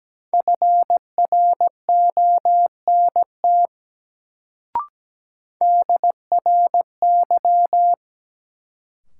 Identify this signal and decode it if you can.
Morse code, heard as a keyed tone, about 17 wpm, 700 Hz. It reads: FRONT DRY
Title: Sentences from Top 500 Words - New Word Drill - 17wpm